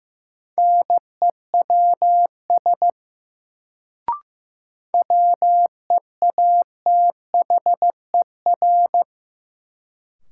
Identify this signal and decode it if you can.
Morse code, heard as a keyed tone, about 15 wpm, 700 Hz. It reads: NEWS WEATHER
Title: Sentences from Top 800 Words - New Word Drill - 15wpm